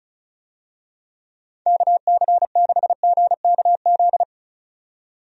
Morse code, heard as a keyed tone, about 36 wpm, 700 Hz. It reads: KC6GKZ
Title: USA Call Signs - 6 Characters - 35wpm